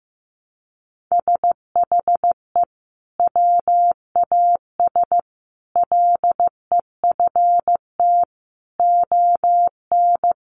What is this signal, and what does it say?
Morse code, heard as a keyed tone, about 15 wpm, 700 Hz. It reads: SHE WAS LEFT ON
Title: Sentences from Top 700 Words 15wpm